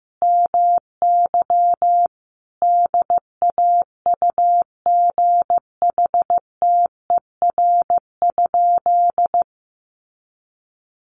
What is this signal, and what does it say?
Morse code, heard as a keyed tone, about 15 wpm, 700 Hz. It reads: MY DAUGHTER?
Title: Sentences from Top 1500 Words 15wpm